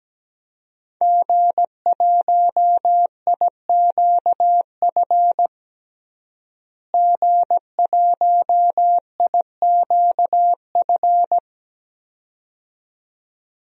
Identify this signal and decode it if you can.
Morse code, heard as a keyed tone, about 17 wpm, 700 Hz. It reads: G1IQF G1IQF
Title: DXCC-100 Call Signs - 5 Characters - Repeated 2x - 17wpm